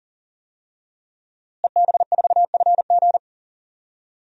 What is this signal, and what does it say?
Morse code, heard as a keyed tone, about 40 wpm, 700 Hz. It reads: EB4FG
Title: DXCC-100 Call Signs - 5 Characters - Repeated 3x - 40wpm